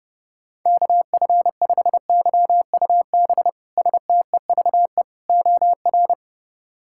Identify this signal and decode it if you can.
Morse code, heard as a keyed tone, about 32 wpm, 700 Hz. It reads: KF5YUB STEVE OR
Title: K1USN SST QSO Practice - 30wpm